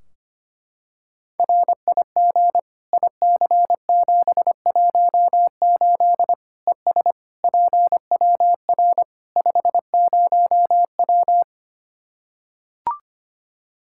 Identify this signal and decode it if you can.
Morse code, heard as a keyed tone, about 25 wpm, 700 Hz. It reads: RIG IC718 ES PWR 50W
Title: Practice QSO - Exchange 2 - Info - 25wpm